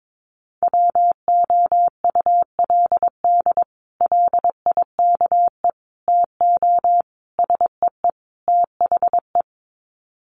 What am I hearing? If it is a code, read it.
Morse code, heard as a keyed tone, about 22 wpm, 700 Hz. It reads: WOULD LIKE TO SEE THE
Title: Sets of 5 Words Encore2 22wpm